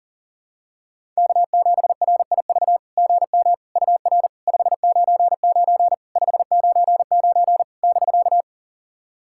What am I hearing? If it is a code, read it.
Morse code, heard as a keyed tone, about 40 wpm, 690 Hz. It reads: K7RIV GM UR 599 599 <BK>
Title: SOTA QSO Practice 40wpm